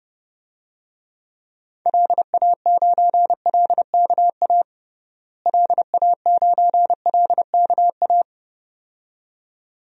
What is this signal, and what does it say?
Morse code, heard as a keyed tone, about 30 wpm, 700 Hz. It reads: LA9LKA LA9LKA
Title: DXCC-100 Call Signs - 6 Characters - Repeated 2x - 30wpm